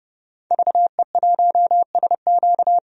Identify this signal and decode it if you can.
Morse code, heard as a keyed tone, about 30 wpm, 700 Hz. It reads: VE1SQ
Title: DXCC-100 Call Signs - 5 Characters - 30wpm